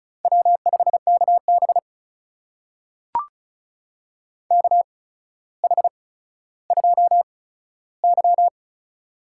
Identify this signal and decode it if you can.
Morse code, heard as a keyed tone, about 36 wpm, spaced as for 15 wpm, 700 Hz. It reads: W5KB KH2Y
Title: USA Call Signs - 4 Characters - 8x Character Spacing - 35wpm